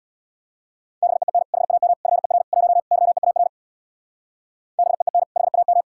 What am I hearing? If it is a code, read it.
Morse code, heard as a keyed tone, about 62 wpm, 700 Hz. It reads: BEEN HAM FER 1 YEAR BEEN HAM
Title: Speed-Racing Common QSO Phrases - 50wpm